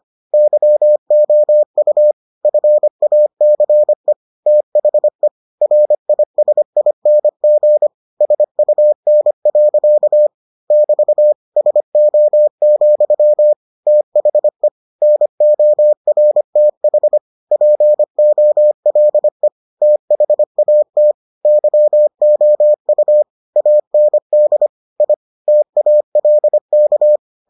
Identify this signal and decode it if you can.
Morse code, heard as a keyed tone, about 25 wpm, 600 Hz. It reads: YOU FACE THE RISING SUN. <BT> SO, THE NORTH POLE THAT YOU AND I TALK